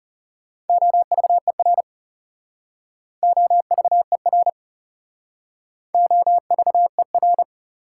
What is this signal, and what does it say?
Morse code, heard as a keyed tone, about 37 wpm, 700 Hz. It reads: OVER OVER OVER
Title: Speed-Racing Prefixes and Suffixes 30wpm